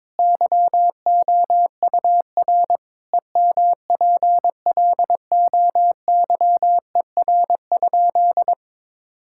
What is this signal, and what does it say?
Morse code, heard as a keyed tone, about 22 wpm, 700 Hz. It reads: YOUR EMPLOYER?
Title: Sentences from Top 1600 Words 22wpm